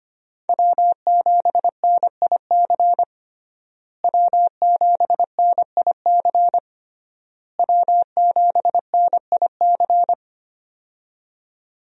Morse code, heard as a keyed tone, about 25 wpm, 700 Hz. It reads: W7NIC W7NIC W7NIC
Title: USA Call Signs - 5 Characters - Repeated 3x - 25wpm